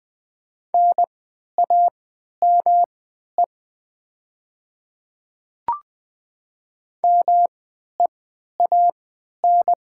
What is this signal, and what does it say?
Morse code, heard as a keyed tone, about 20 wpm, spaced as for 11 wpm, 700 Hz. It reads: NAME MEAN
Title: 3x Character Spacing - Top 300 Words - 20wpm